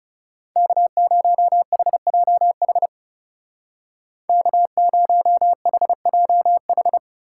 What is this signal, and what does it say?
Morse code, heard as a keyed tone, about 37 wpm, 700 Hz. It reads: K0HJH K0HJH
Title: Speed Racing - USA 1x3 Call Signs 30wpm